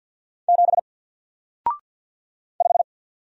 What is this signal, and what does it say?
Morse code, heard as a keyed tone, about 50 wpm, 700 Hz. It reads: 7 5
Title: Single Letter-Number Alt 50wpm